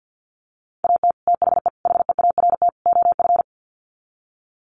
Morse code, heard as a keyed tone, about 48 wpm, 700 Hz. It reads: AT THE HEART OF
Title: Sets of 4 Words 50wpm